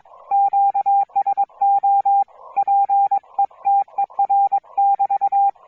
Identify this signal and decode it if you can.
Morse code, heard as a keyed tone, about 22 wpm, 810 Hz. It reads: QSO PETER<BT>